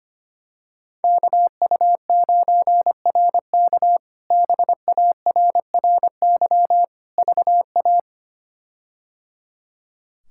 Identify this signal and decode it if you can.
Morse code, heard as a keyed tone, about 25 wpm, 700 Hz. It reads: KU9RK BARRY VA
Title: K1USN SST QSO Practice - 25wpm